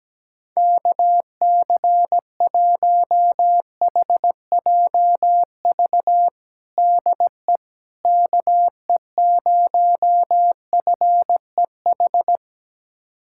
Morse code, heard as a keyed tone, about 17 wpm, 700 Hz. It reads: KC1HJV DE KE0FEH
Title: USA Callsign DE Callsign 17wpm